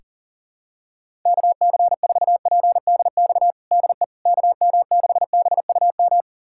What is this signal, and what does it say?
Morse code, heard as a keyed tone, about 40 wpm, 700 Hz. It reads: KC4PDX DE KM6BUM
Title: USA Callsign DE Callsign 40wpm